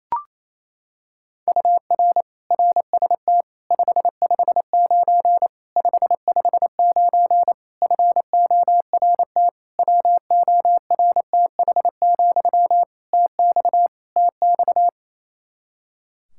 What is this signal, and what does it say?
Morse code, heard as a keyed tone, about 27 wpm, 700 Hz. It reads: UR RST 559 559 FORT WORTH, TX TX
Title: Practice QSO - Exchange 1 - Beginning - 28wpm